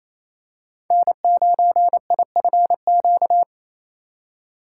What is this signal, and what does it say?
Morse code, heard as a keyed tone, about 28 wpm, 700 Hz. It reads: N9IFQ
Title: USA Call Signs - 5 Characters - 4x Character Spacing - 28wpm